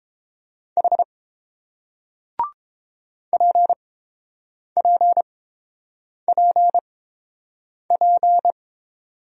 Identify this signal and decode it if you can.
Morse code, heard as a keyed tone, about 30 wpm, spaced as for 7 wpm, 700 Hz. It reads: H PPPP